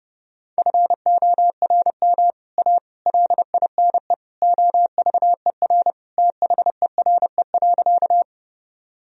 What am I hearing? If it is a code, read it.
Morse code, heard as a keyed tone, about 30 wpm, 700 Hz. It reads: FORM A LINE OVER THERE.
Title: Sentences from Top 1000 Words - Review Part 1 - 30wpm